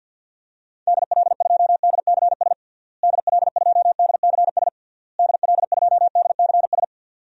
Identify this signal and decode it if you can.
Morse code, heard as a keyed tone, about 52 wpm, 700 Hz. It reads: DL2DCS DL2DCS DL2DCS
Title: DXCC-100 Call Signs - 6 Characters - Repeated 3x - 50wpm